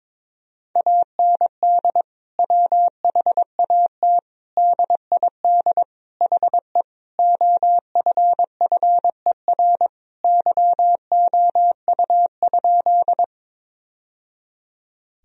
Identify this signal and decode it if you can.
Morse code, heard as a keyed tone, about 22 wpm, 700 Hz. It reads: AND WHAT DID HE OFFER YOU?
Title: Sentences from Top 1000 Words - Review Part 4 - 22wpm